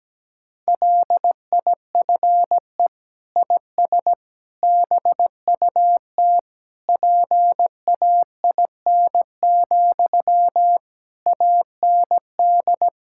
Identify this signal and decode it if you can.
Morse code, heard as a keyed tone, about 17 wpm, 700 Hz. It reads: LIFE IS BUT PAIN, AND
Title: Sentences from Top 1600 Words 17wpm